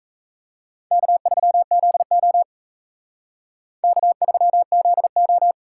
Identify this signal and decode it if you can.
Morse code, heard as a keyed tone, about 32 wpm, 700 Hz. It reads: K3ZO K3ZO
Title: Speed Racing - USA 1x2 Call Signs 28wpm